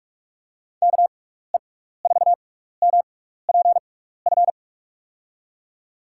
Morse code, heard as a keyed tone, about 46 wpm, spaced as for 16 wpm, 700 Hz. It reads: KE4MPF
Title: USA Call Signs - 6 Characters - 6x Character Spacing - 45wpm